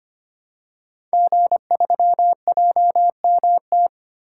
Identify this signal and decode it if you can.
Morse code, heard as a keyed tone, about 25 wpm, 700 Hz. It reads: G3JMT
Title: Code Groups - 5 Letter:Numbers - 25wpm